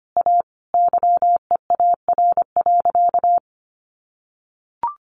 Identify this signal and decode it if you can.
Morse code, heard as a keyed tone, about 25 wpm, 700 Hz. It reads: A YEAR.
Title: Sentences from Top 700 Words 25wpm